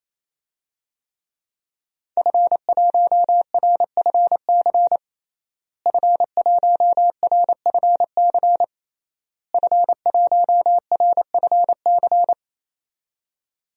Morse code, heard as a keyed tone, about 28 wpm, 700 Hz. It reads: F1RFC F1RFC F1RFC
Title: DXCC-100 Call Signs - 5 Characters - Repeated 3x - 28wpm